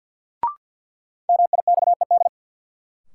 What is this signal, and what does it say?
Morse code, heard as a keyed tone, about 47 wpm, 700 Hz. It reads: MIXED
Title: Sentences from Top 1500 Words - New Word Drill Alt - 50wpm